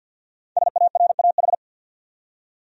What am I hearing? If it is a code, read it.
Morse code, heard as a keyed tone, about 53 wpm, 700 Hz. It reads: SARAH
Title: USA Common Names - 50wpm